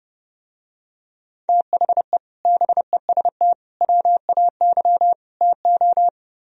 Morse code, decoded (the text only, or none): THE BEST WAY TO